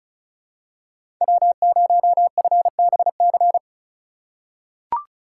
Morse code, heard as a keyed tone, about 34 wpm, 700 Hz. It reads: W0FBC
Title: USA Call Signs - 2x Character Spacing - 35wpm